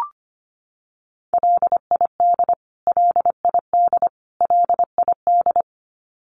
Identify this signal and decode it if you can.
Morse code, heard as a keyed tone, about 25 wpm, 700 Hz. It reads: LID LID LID
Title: Abbreviations Repeated 3x - 25wpm